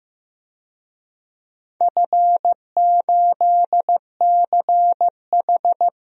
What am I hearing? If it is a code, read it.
Morse code, heard as a keyed tone, about 15 wpm, 700 Hz. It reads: F8CH
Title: DXCC-100 Call Signs - 4 Characters - Repeated 3x - 15wpm